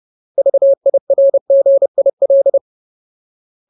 Morse code, heard as a keyed tone, about 30 wpm, 550 Hz. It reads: VIRGIL
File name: Ultimate Word List / 30WPM Ultimate